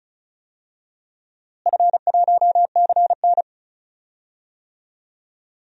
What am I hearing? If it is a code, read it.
Morse code, heard as a keyed tone, about 36 wpm, 700 Hz. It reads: F1CN
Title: DXCC-100 Call Signs - 4 Characters - Repeated 3x - 35wpm